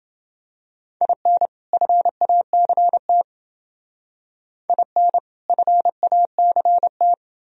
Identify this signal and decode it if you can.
Morse code, heard as a keyed tone, about 29 wpm, 700 Hz. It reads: IN FACT IN FACT